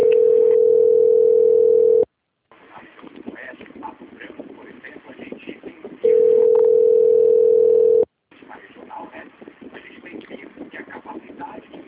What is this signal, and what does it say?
Ringback tone (North America), 440 + 480 Hz.
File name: out- - - - - .28330